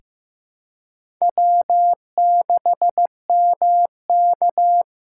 Morse code, heard as a keyed tone, about 15 wpm, 700 Hz. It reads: W6MK
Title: USA Call Signs - 4 Characters - 2x Character Spacing - 15wpm